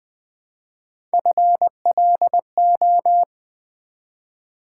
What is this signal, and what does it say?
Morse code, heard as a keyed tone, about 20 wpm, 700 Hz. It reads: FLO
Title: Build A Word - 1 Letter At A Time - 5-Letter Words - 20wpm